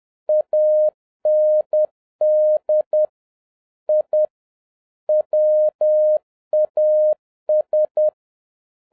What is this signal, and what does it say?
Morse code, heard as a keyed tone, about 10 wpm, 610 Hz. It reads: AND I WAS